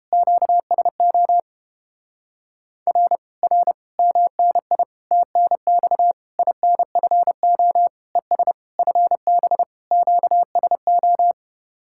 Morse code, heard as a keyed tone, about 29 wpm, 700 Hz. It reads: QSO R R MNI TNX INFO ES FB QSO